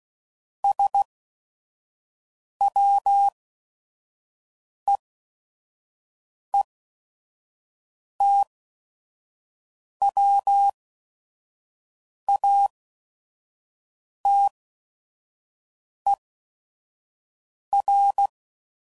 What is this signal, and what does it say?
Morse code, heard as a keyed tone, about 16 wpm, spaced as for 5 wpm, 800 Hz. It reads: SWEETWATER
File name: CW Code Groups @5WPM / fromSM's / Cities